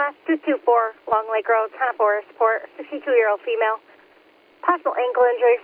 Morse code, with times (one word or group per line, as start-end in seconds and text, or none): none